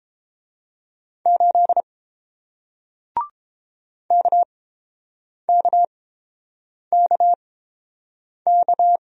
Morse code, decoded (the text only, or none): 8 KKKK